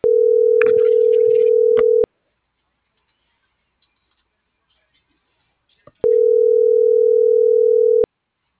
Ringback tone (North America), 440 + 480 Hz.
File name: CENTRAL MAERSK / out isabel.bravo - .70118